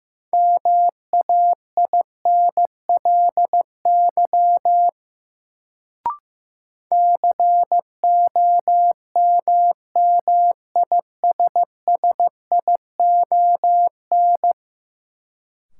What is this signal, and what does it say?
Morse code, heard as a keyed tone, about 15 wpm, 700 Hz. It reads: MAINLY COMMISSION
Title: Sentences from Top 1400 Words - New Word Drill Alt - 15wpm